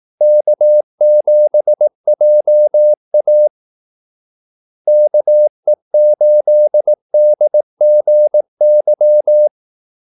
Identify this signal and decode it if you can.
Morse code, heard as a keyed tone, about 18 wpm, 600 Hz. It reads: K7JA KE8DGY
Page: Kaldesignaler 18 wpm